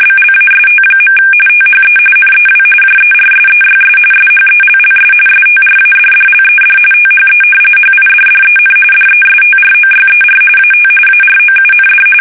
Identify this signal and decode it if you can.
RTTY, about 45.45 baud. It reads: BLAST. THE NEW JERSEYS SHELLS STRADDLED THE HIDDEN DESTROYER AT AROUND
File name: fsk850 / battleshipnj